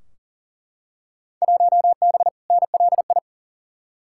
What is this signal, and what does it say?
Morse code, heard as a keyed tone, about 38 wpm, 700 Hz. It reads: 1B NLI
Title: Field Day Exchanges Encore - 40wpm